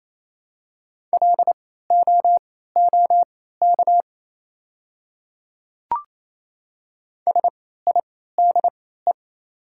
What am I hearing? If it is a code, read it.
Morse code, heard as a keyed tone, about 27 wpm, spaced as for 16 wpm, 700 Hz. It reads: LOOK SIDE
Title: 3x Character Spacing - Top 200 Words - 28wpm